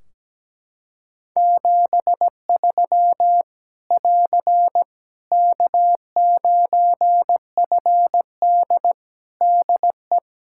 Morse code, heard as a keyed tone, about 17 wpm, 700 Hz. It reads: 73 <AR> K9FD DE
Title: K1USN SST QSO Practice 17wpm